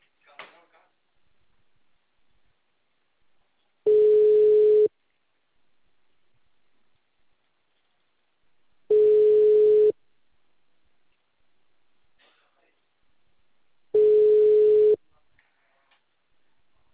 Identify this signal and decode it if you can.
Ringback tone (Europe, 425 Hz), with speech.
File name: out- -2024- - - .89441